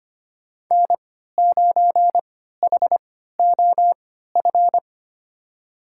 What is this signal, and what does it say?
Morse code, heard as a keyed tone, about 25 wpm, spaced as for 14 wpm, 700 Hz. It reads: N9HOF
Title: USA Call Signs - 5 Characters - 3x Character Spacing - 25wpm